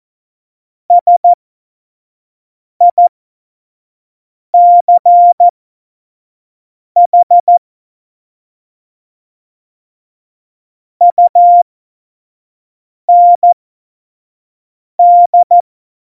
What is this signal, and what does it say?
Morse code, heard as a keyed tone, about 14 wpm, spaced as for 5 wpm, 700 Hz. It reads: SICH UND
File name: Morsekurs / Witze / Witz 15. Mai 2018, 25 F70